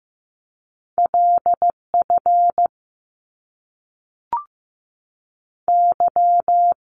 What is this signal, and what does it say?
Morse code, heard as a keyed tone, about 15 wpm, 700 Hz. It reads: LF Y
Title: Top Two Letter Combinations in Words 15wpm